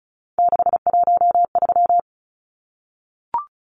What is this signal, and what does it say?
Morse code, heard as a keyed tone, about 34 wpm, 700 Hz. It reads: 613
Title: Numbers - 3-Digit 35wpm